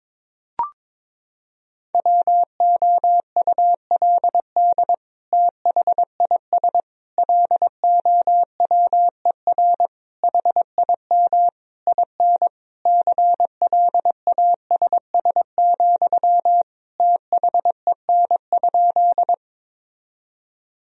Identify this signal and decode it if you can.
Morse code, heard as a keyed tone, about 21 wpm, 700 Hz. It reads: WOULD THIS LOWER HIM IN CLASS, THEN?
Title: Sentences from Top 1000 Words - Review Part 4 - 22wpm